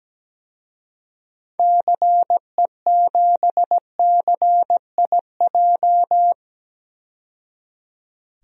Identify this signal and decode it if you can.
Morse code, heard as a keyed tone, about 17 wpm, 700 Hz. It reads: CE7CIJ
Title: DXCC-100 Call Signs - 6 Characters - 17wpm